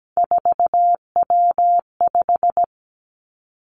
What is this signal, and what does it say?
Morse code, heard as a keyed tone, about 17 wpm, 700 Hz. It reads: 4W5
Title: Three Challenging Letter-Number Combinations 17wpm